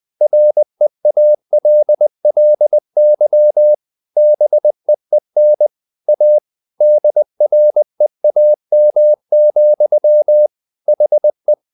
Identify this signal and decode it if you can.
Morse code, heard as a keyed tone, about 20 wpm, 600 Hz. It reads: REALLY BEEN A DREAM, HE
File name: the old man and the sea part 2 20wpm